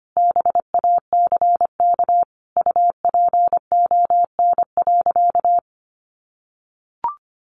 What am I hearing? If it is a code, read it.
Morse code, heard as a keyed tone, about 25 wpm, 700 Hz. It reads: BACK UPON.
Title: Sentences from Top 1100 Words 25wpm